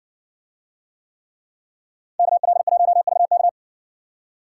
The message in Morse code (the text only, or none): KL1VK